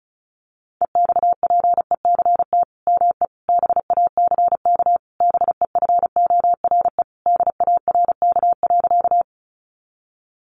Morse code, heard as a keyed tone, about 34 wpm, 700 Hz. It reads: EXPECT ME BACK BEFORE DARK.
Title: Sentences from Top 1100 Words 35wpm